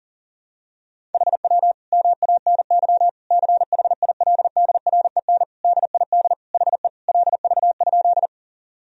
Morse code, heard as a keyed tone, about 40 wpm, 700 Hz. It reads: HW MANY CHILDREN DID HE LV?
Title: Sentences from Top 400 Words and Abbreviations 40wpm